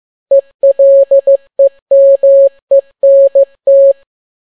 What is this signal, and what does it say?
Morse code, heard as a keyed tone, about 15 wpm, 550 Hz. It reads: ELEMENT